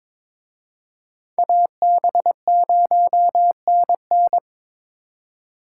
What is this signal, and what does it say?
Morse code, heard as a keyed tone, about 26 wpm, 700 Hz. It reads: AB0NN
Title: Speed Racing - USA 2x2 Call Signs 22wpm